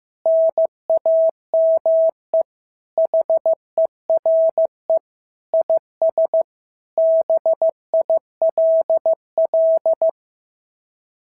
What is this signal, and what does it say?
Morse code, heard as a keyed tone, about 14 wpm, 650 Hz. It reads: NAME HERE IS BILL
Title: SOTA QSO Practice 15wpm